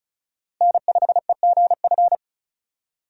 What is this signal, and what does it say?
Morse code, heard as a keyed tone, about 34 wpm, 700 Hz. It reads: N5EGF
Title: USA Call Signs - 5 Characters - Repeated 2x - 35wpm